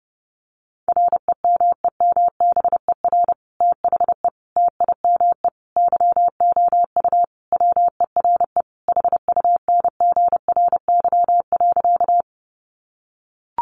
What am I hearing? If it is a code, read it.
Morse code, heard as a keyed tone, about 30 wpm, 700 Hz. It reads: REMEMBER THE TIME YOU WERE HUNGRY.
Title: Sentences from Top 1400 Words 30wpm